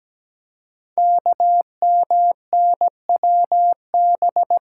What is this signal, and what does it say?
Morse code, heard as a keyed tone, about 17 wpm, 700 Hz. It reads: KMNWB
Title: Code Groups - 5 Letters - 17wpm